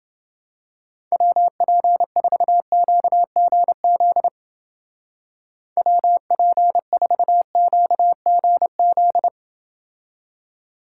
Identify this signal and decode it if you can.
Morse code, heard as a keyed tone, about 29 wpm, 700 Hz. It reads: WP4QGZ WP4QGZ